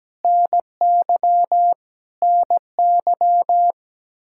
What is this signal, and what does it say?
Morse code, heard as a keyed tone, about 17 wpm, 700 Hz. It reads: NY NY
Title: Practice QSO - Exchange 1 - Beginning - 17wpm